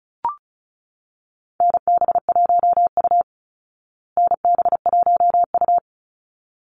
Morse code, heard as a keyed tone, about 34 wpm, 700 Hz. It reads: NB1U NB1U
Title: Call Signs Repeated 3x - 35wpm